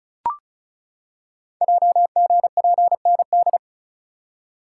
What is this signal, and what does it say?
Morse code, heard as a keyed tone, about 33 wpm, 700 Hz. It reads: JGPND
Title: Code Groups - 5 Letters - 35wpm